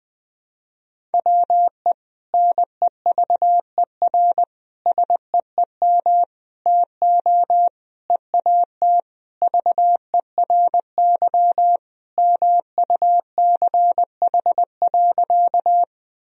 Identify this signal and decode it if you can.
Morse code, heard as a keyed tone, about 20 wpm, 700 Hz. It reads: WE NEVER SEEM TO EAT VERY MUCH.
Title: Sentences from Top 1000 Words - Review Part 1 - 20wpm